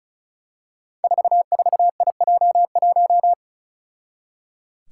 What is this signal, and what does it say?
Morse code, heard as a keyed tone, about 36 wpm, 700 Hz. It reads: 44IJ1
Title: Five Challenging Letter-Number Combinations 35wpm